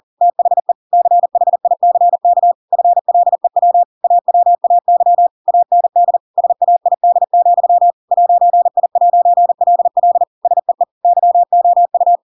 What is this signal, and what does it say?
Morse code, heard as a keyed tone, about 40 wpm, 700 Hz. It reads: THE CHICK FLEW AWAY AND SAID, 'I'LL SEE YOU
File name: Pinocchio / Pinocchio ch7 40 40 ik1hge